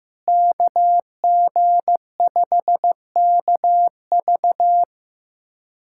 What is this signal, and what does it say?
Morse code, heard as a keyed tone, about 15 wpm, 700 Hz. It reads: KG5KV
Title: USA Callsign DE Callsign - 5 Characters - 15wpm